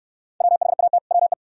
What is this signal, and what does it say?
Morse code, heard as a keyed tone, about 66 wpm, 690 Hz. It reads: WHAT WE